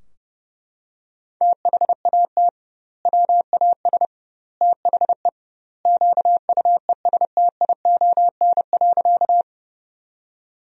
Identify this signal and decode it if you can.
Morse code, heard as a keyed tone, about 32 wpm, 700 Hz. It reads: THAT WAS THE QUESTION.
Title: 2x Word-Spacing - Sentences from Top 400 Words - 30wpm